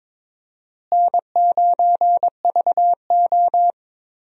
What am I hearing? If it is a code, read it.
Morse code, heard as a keyed tone, about 22 wpm, 700 Hz. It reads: N9VO
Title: USA Call Signs - 4 Characters - Repeated 2x - 22wpm